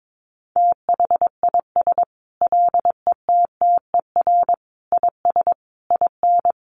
Morse code, heard as a keyed tone, about 22 wpm, 700 Hz. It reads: THIS LETTER IS IN
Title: Sentences from Top 1400 Words 22wpm